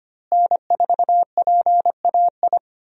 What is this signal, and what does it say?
Morse code, heard as a keyed tone, about 25 wpm, 700 Hz. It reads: N4PAI
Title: USA Call Signs - 5 Characters - Repeated 2x - 25wpm